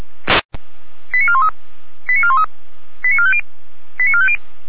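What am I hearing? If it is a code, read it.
Five-tone selective call (ZVEI), heard as quick runs of five tones, tones of about 70 ms. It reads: 89413 89413 89450 89450
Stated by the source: ZVEI five-tone call received by SDR, from the feuerwehr-orlamuende.org fire brigade audio server.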